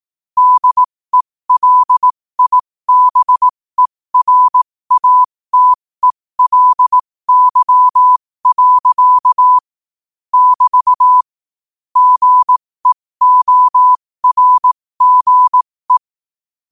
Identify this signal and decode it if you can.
Morse code, heard as a keyed tone, about 18 wpm, spaced as for 15 wpm, 1000 Hz. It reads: DELIBERATELY. <BT> GEORGE
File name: qotdfiles / 15wpm